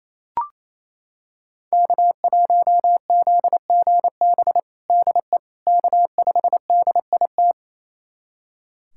Morse code, heard as a keyed tone, about 28 wpm, 700 Hz. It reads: K1ZGB DE K5DIT
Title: USA Callsign DE Callsign - 5 Characters - 28wpm